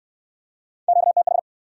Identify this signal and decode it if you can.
Morse code, heard as a keyed tone, about 69 wpm, 700 Hz. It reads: QTH